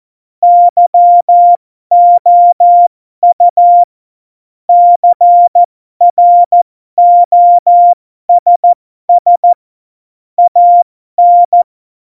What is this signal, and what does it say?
Morse code, heard as a keyed tone, about 14 wpm, spaced as for 12 wpm, 700 Hz. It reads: YOU CROSS AN
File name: Morsekurs / Witze